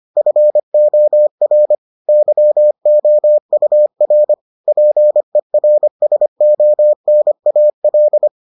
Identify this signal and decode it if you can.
Morse code, heard as a keyed tone, about 25 wpm, 600 Hz. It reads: FOR YOUR PERSONAL